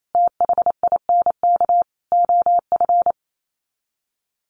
Morse code, heard as a keyed tone, about 28 wpm, 700 Hz. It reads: THINK OF
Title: Sets of 5 Words Encore2 28wpm